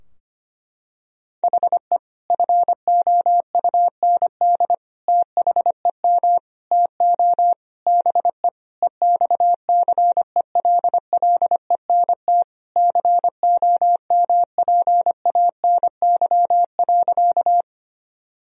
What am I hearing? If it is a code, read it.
Morse code, heard as a keyed tone, about 25 wpm, 700 Hz. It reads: HE FOUND THEM TO BE EXCELLENT COMPANY.
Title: Sentences from Top 1000 Words - Review Part 3 - 25wpm